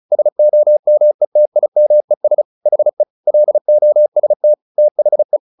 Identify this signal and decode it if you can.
Morse code, heard as a keyed tone, about 34 wpm, 600 Hz. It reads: SOMETIMES HE LOST THE
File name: the old man and the sea 2 35wpm